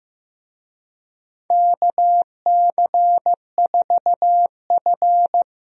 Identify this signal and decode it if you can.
Morse code, heard as a keyed tone, about 15 wpm, 700 Hz. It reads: KC4F
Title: USA Call Signs - 4 Characters - Repeated 2x - 15wpm